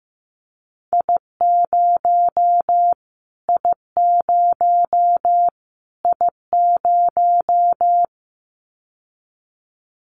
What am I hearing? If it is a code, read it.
Morse code, heard as a keyed tone, about 15 wpm, 700 Hz. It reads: I0 I0 I0
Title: Callsign Prefixes for Top 25 DXCC - Repeated 3x - 15wpm